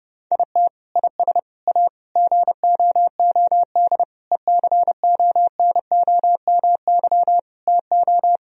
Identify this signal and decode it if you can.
Morse code, heard as a keyed tone, about 30 wpm, 700 Hz. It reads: IT IS A GOOD ECONOMY TO
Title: Sentences from Top 1000 Words - Review Part 3 - 30wpm